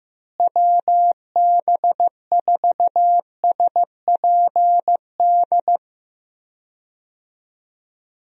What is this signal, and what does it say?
Morse code, heard as a keyed tone, about 15 wpm, 700 Hz. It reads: WB4SPD
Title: USA Call Signs - 6 Characters - Repeated 2x - 15wpm